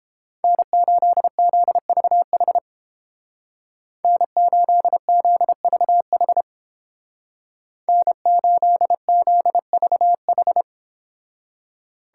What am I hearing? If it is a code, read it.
Morse code, heard as a keyed tone, about 29 wpm, 700 Hz. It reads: N8ZVH N8ZVH N8ZVH